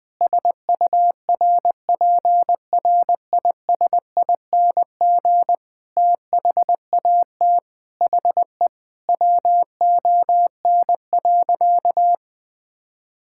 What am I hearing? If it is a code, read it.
Morse code, heard as a keyed tone, about 20 wpm, 700 Hz. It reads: SURPRISING THAT HE WON.
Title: Sentences with 5-Word Phrases - 20wpm